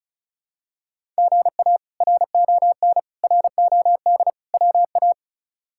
Morse code, heard as a keyed tone, about 36 wpm, 700 Hz. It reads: GA RON ROD WA
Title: K1USN SST QSO Practice - 35wpm